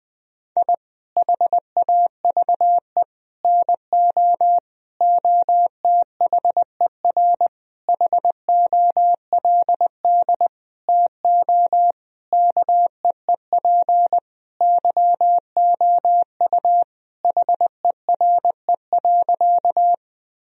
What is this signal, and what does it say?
Morse code, heard as a keyed tone, about 20 wpm, 700 Hz. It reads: I HAVE NO OTHER HOLD TO KEEP YOU HERE.
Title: Sentences from Top 1000 Words - Review Part 2 - 20wpm